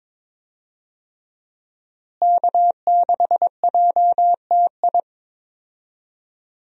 Morse code, heard as a keyed tone, about 22 wpm, 700 Hz. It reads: K6JTI
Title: USA Call Signs - 5 Characters - Repeated 3x - 22wpm